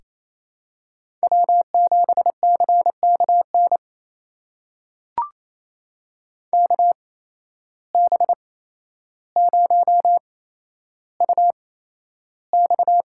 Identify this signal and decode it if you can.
Morse code, heard as a keyed tone, about 28 wpm, spaced as for 12 wpm, 700 Hz. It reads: W7CKN KB0UX
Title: USA Call Signs - 5 Characters - 8x Character Spacing - 28wpm